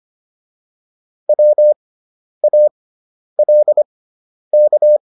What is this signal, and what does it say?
Morse code, heard as a keyed tone, about 25 wpm, spaced as for 10 wpm, 600 Hz. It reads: WALK